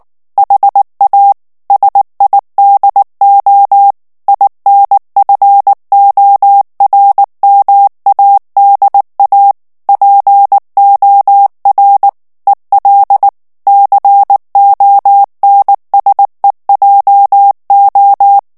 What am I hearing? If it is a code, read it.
Morse code, heard as a keyed tone, about 19 wpm, 800 Hz. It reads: HA SIDO INFORMADA POR EL CONSEJO
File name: morse mp3 / ESPAGNOL TEXTE EN CLAIR 2 VITESSE 18 MOTS MINUTES